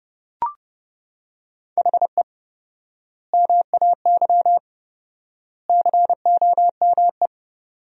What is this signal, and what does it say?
Morse code, heard as a keyed tone, about 30 wpm, 700 Hz. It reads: HE MAY COME
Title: 4x Word-Spacing - Sentences from Top 300 Words - 30wpm